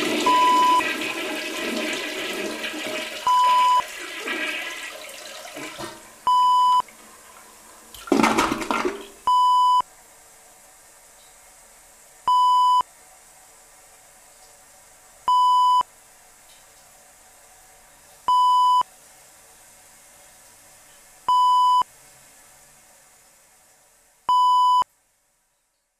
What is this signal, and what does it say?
There is no code.